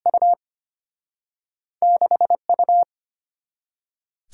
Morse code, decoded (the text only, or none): U 6U